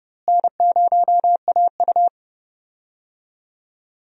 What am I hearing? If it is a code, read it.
Morse code, heard as a keyed tone, about 30 wpm, 700 Hz. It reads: N0AU
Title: USA Call Signs - 4 Characters - Repeated 3x - 30wpm